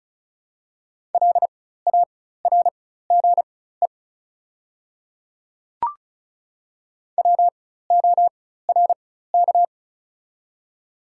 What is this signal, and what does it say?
Morse code, heard as a keyed tone, about 36 wpm, spaced as for 16 wpm, 700 Hz. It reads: LARGE WORK
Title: 4x Character Spacing - Top 200 Words - 35wpm